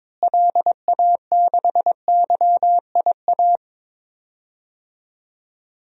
Morse code, heard as a keyed tone, about 21 wpm, 700 Hz. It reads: LA6YIA
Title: DXCC-100 Call Signs - 6 Characters - Repeated 3x - 22wpm